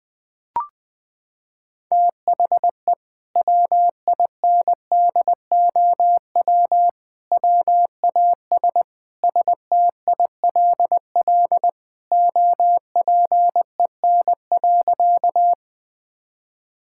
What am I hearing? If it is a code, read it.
Morse code, heard as a keyed tone, about 21 wpm, 700 Hz. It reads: THE WINDOW WAS STILL OPEN.
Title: Sentences from Top 1600 Words 20wpm